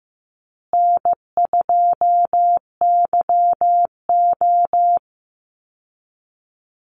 Morse code, heard as a keyed tone, about 15 wpm, 700 Hz. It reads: N2YO
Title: Easy Call Signs Repeated 3x - 15wpm